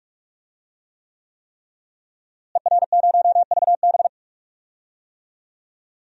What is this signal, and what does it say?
Morse code, heard as a keyed tone, about 41 wpm, 700 Hz. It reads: ER0VB
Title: DXCC-100 Call Signs - 5 Characters - Repeated 3x - 45wpm